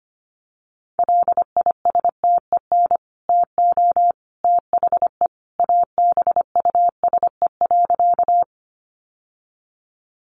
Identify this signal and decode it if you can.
Morse code, heard as a keyed tone, about 25 wpm, 700 Hz. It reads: LISTEN TO THE ABUSE.
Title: Sentences from Top 1300 Words 25wpm